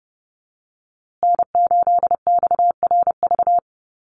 Morse code, heard as a keyed tone, about 30 wpm, 700 Hz. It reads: N8XRV
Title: USA Call Signs Repeated 3x 30wpm